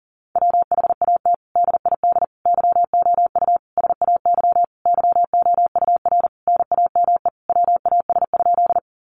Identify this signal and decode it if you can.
Morse code, heard as a keyed tone, about 40 wpm, 700 Hz. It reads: WHAT DID YOU SAY YOUR NAME WAS?
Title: Sentences from Top 200 Words 40wpm